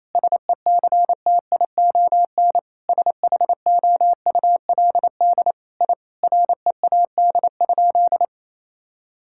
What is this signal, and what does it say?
Morse code, heard as a keyed tone, about 27 wpm, 700 Hz. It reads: SECTION SHOULD I READ?
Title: Sentences from Top 1000 Words - Review Part 5 - 28wpm